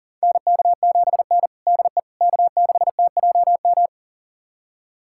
Morse code, heard as a keyed tone, about 40 wpm, 700 Hz. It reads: NK7N DE K6TJM
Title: USA Callsign DE Callsign 40wpm